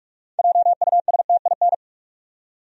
Morse code, heard as a keyed tone, about 41 wpm, 700 Hz. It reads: JUSTIN
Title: USA Common Names - 45wpm